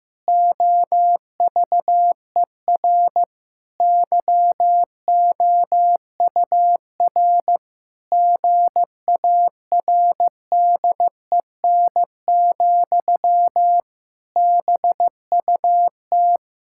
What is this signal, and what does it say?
Morse code, heard as a keyed tone, about 15 wpm, 700 Hz. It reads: OVER YOUR GARDEN, BUT
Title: Pangram Challenge v2 - 15wpm